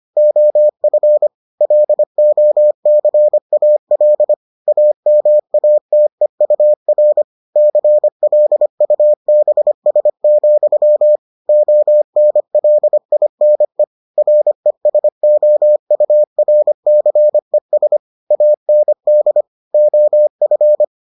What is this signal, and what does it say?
Morse code, heard as a keyed tone, about 25 wpm, 600 Hz. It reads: OF LOCAL AMATEUR CLUBS, ONLINE RESOURCES AND OF